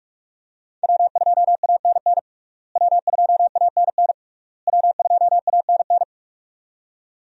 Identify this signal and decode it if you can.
Morse code, heard as a keyed tone, about 46 wpm, 700 Hz. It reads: W2ANN W2ANN W2ANN
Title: USA Call Signs - 5 Characters - Repeated 3x - 45wpm